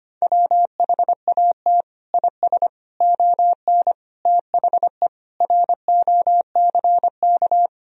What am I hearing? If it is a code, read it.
Morse code, heard as a keyed tone, about 25 wpm, 700 Hz. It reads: WHAT IS ON THE ROCK
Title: Sentences from Top 1000 Words - Review Part 2 - 25wpm